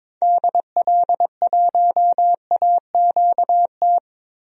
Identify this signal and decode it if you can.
Morse code, heard as a keyed tone, about 22 wpm, 700 Hz. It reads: DL1AQT
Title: DXCC-100 Call Signs - 6 Characters - 22wpm